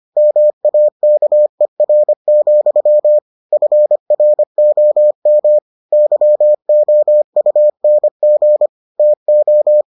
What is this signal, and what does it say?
Morse code, heard as a keyed tone, about 25 wpm, 600 Hz. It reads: MAKER, FROM YOUNG TO